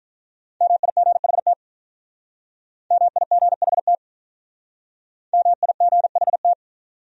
Morse code, heard as a keyed tone, about 42 wpm, 700 Hz. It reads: MIGHT MIGHT MIGHT
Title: Speed-Racing Top 500 Words 35wpm